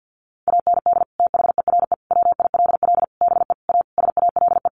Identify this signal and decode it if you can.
Morse code, heard as a keyed tone, about 48 wpm, 700 Hz. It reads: AND THERE WILL BE A SALE
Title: Sentences from Top 1400 Words 50wpm